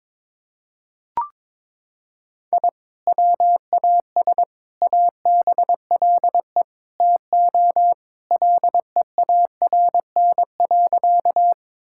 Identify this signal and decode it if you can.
Morse code, heard as a keyed tone, about 22 wpm, 700 Hz. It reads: I WAS ABLE TO LEARN.
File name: practice / 22wpm / Sentences with 4-Word Phrases - 22wpm